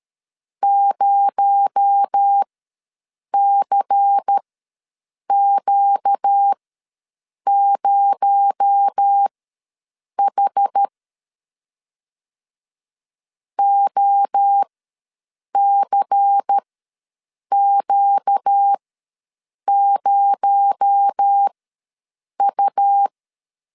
Morse code, heard as a keyed tone, about 13 wpm, spaced as for 7 wpm, 800 Hz. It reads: 0CQ0H OCQ0U
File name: LEZIONI / 6 PKX0 / 6 PKX0 ws 12wpm 3448